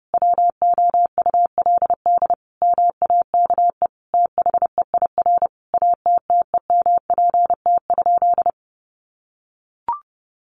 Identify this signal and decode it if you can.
Morse code, heard as a keyed tone, about 30 wpm, 700 Hz. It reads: WOULD MAKE THEIR ATTEMPT?
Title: Sentences from Top 1000 Words 30wpm